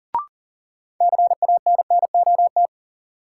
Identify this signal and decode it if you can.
Morse code, heard as a keyed tone, about 40 wpm, 700 Hz. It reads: CANNOT
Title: Sentences from Top 1000 Words - Review Part 6 - New Word Drill - 40wpm